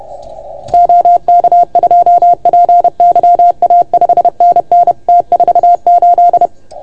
Morse code, heard as a keyed tone, about 30 wpm, 690 Hz. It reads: OK2PYA 5NN T48
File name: Archiv / CUC381 / OK1FGD CUC 381